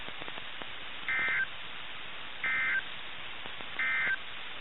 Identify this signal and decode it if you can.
Emergency Alert System (SAME) tones, heard as harsh buzzing bursts. It is heard from 1.0 to 4.2 s.
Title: NOAA Weather Radio - KIH61